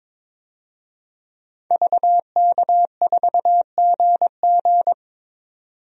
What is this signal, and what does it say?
Morse code, heard as a keyed tone, about 22 wpm, 700 Hz. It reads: VK4GG
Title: DXCC-100 Call Signs - 5 Characters - 22wpm